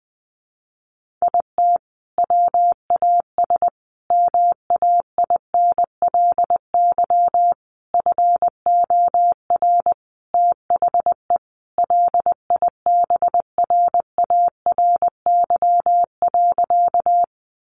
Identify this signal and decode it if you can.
Morse code, heard as a keyed tone, about 19 wpm, 700 Hz. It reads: IT WAS MAINLY FOR THE LIBRARY.
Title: Sentences from Top 1400 Words 20wpm